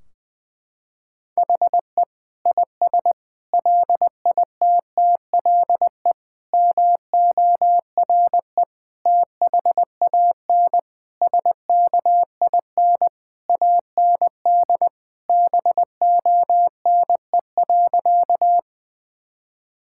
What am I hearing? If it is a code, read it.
Morse code, heard as a keyed tone, about 21 wpm, 700 Hz. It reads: HE IS LITTLE MORE THAN SKIN AND BONE.
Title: Sentences from Top 1600 Words 20wpm